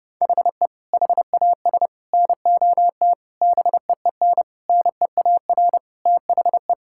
Morse code, heard as a keyed tone, about 30 wpm, 700 Hz. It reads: HE HAS NOT BEEN NEAR THE
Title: Sentences from Top 1000 Words - Review Part 6 - 30wpm